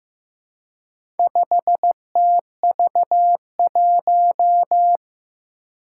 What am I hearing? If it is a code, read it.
Morse code, heard as a keyed tone, about 15 wpm, 700 Hz. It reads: 5TV1
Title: Four Challenging Letter-Number Combinations 15wpm